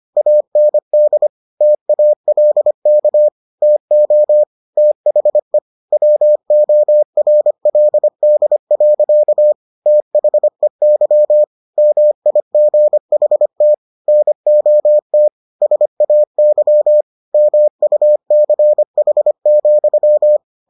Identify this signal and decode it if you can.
Morse code, heard as a keyed tone, about 25 wpm, 600 Hz. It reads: AND TALK TO THE WORLD. THEY MIGHT NOT SAY MUCH,